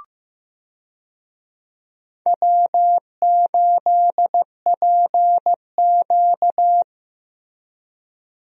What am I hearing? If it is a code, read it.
Morse code, heard as a keyed tone, about 15 wpm, 700 Hz. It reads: W8PQ
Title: USA Call Signs - 4 Characters - Repeated 2x - 15wpm